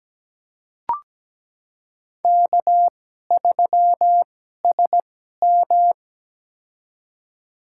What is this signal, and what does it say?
Morse code, heard as a keyed tone, about 17 wpm, spaced as for 12 wpm, 700 Hz. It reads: K3SM
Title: USA Call Signs - 4 Characters - 2x Character Spacing - 17wpm